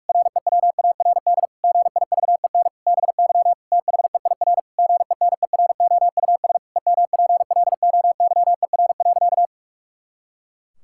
Morse code, heard as a keyed tone, about 46 wpm, 700 Hz. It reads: REWARD GIVEN BY THEIR GENEROUS EMPLOYER.
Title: Pangram Challenge v2 - 45wpm